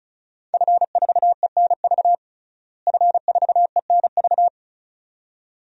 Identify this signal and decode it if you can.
Morse code, heard as a keyed tone, about 36 wpm, 700 Hz. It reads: F4ENV F4ENV
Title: DXCC-100 Call Signs - 5 Characters - Repeated 2x - 35wpm